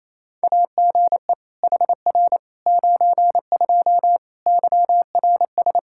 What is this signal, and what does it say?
Morse code, heard as a keyed tone, about 28 wpm, 700 Hz. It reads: AGE HR 92 YRS
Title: Practice QSO - Exchange 3 - More Info - 28wpm